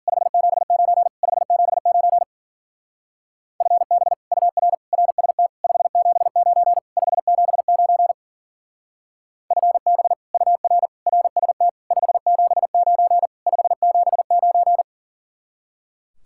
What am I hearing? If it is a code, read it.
Morse code, heard as a keyed tone, about 49 wpm, 700 Hz. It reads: 579 579 FB UR RST 579 579 FB UR RST 579 579
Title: Speed-Racing Common QSO Phrases Encore - 40wpm